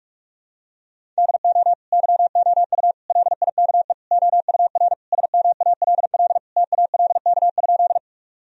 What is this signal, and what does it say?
Morse code, heard as a keyed tone, about 46 wpm, 700 Hz. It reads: DO YOU LIKE OUR SMALL TALK?
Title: Sentences from Top 1000 Words - Review Part 1 - 45wpm